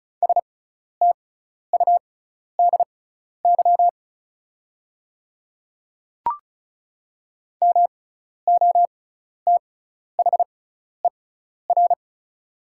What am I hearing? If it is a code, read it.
Morse code, heard as a keyed tone, about 34 wpm, spaced as for 12 wpm, 700 Hz. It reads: STUDY MOTHER
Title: 6x Character Spacing - Top 300 Words - 35wpm